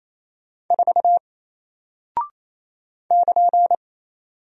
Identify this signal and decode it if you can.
Morse code, heard as a keyed tone, about 28 wpm, 700 Hz. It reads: 4 <KN>
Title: Single Character (Letter:Number:Punctuation:Prosign) 28wpm